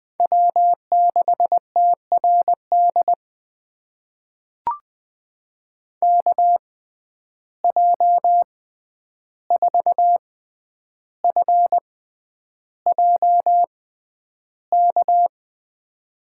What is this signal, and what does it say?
Morse code, heard as a keyed tone, about 21 wpm, spaced as for 10 wpm, 700 Hz. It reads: W6TRD KJ4FJK
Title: USA Call Signs - 6x Character Spacing - 20wpm